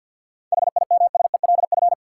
Morse code, heard as a keyed tone, about 53 wpm, 700 Hz. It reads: HIMSELF
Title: Sentences from Top 1500 Words - New Word Drill Alt - 50wpm